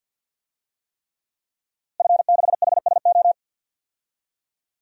Morse code, heard as a keyed tone, about 46 wpm, 700 Hz. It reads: F6HSO
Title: DXCC-100 Call Signs - 5 Characters - Repeated 3x - 50wpm